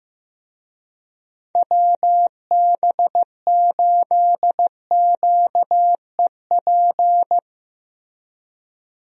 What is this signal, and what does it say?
Morse code, heard as a keyed tone, about 15 wpm, 700 Hz. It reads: WB8QEP
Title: USA Call Signs - 6 Characters - Repeated 2x - 15wpm